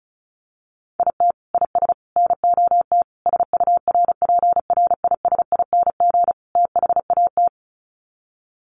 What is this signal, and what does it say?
Morse code, heard as a keyed tone, about 34 wpm, 700 Hz. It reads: IT IS NOT SURPRISING THAT
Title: Sets of 5 Words 35wpm